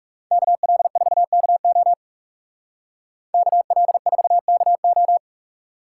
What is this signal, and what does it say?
Morse code, heard as a keyed tone, about 35 wpm, 700 Hz. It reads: KL4KO KL4KO
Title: Speed Racing - USA 2x2 Call Signs 30wpm